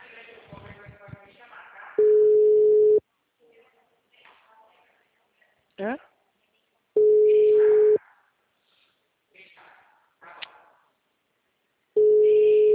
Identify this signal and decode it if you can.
Ringback tone (Europe, 425 Hz), with speech.
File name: out- -2015- - - .101